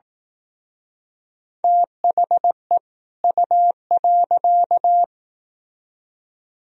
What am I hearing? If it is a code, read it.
Morse code, heard as a keyed tone, about 18 wpm, 700 Hz. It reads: THE U.
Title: Speed-Racing - Sets of 2 Words 15wpm